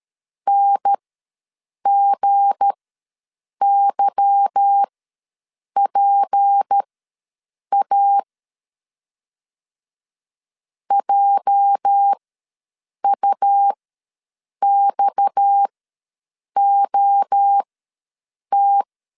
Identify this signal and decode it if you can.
Morse code, heard as a keyed tone, about 13 wpm, spaced as for 7 wpm, 800 Hz. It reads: NGYPA JUXOT
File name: LEZIONI / 8 YJ45 / 8 YJ ws 12wpm 3342